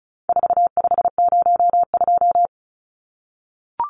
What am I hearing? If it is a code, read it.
Morse code, heard as a keyed tone, about 34 wpm, 700 Hz. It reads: 4502
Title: Numbers - 4-Digit 35wpm